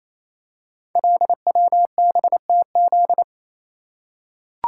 Morse code, heard as a keyed tone, about 27 wpm, 700 Hz. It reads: LWBTZ
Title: Code Groups - 5 Letter:Numbers - 28wpm